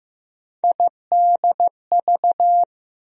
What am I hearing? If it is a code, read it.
Morse code, heard as a keyed tone, about 15 wpm, 700 Hz. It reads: IDV
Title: USA Call Sign Postfixes - 3 Letters 15wpm